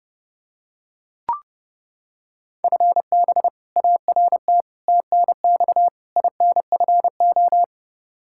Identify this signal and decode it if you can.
Morse code, heard as a keyed tone, about 30 wpm, 700 Hz. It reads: FB ART TNX INFO
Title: K1USN SST QSO Practice 30wpm